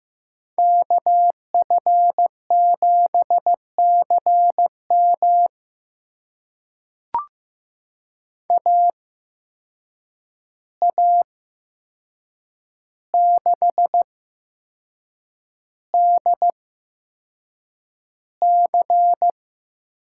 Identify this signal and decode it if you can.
Morse code, heard as a keyed tone, about 15 wpm, spaced as for 6 wpm, 700 Hz. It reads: KF7CM AA6DC
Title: USA Call Signs - 5 Characters - 8x Character Spacing - 15wpm